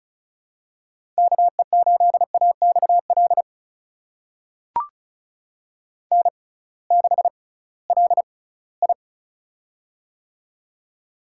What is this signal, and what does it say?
Morse code, heard as a keyed tone, about 36 wpm, spaced as for 21 wpm, 700 Hz. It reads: KE8AXL N6LI
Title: USA Call Signs - 6x Character Spacing - 35wpm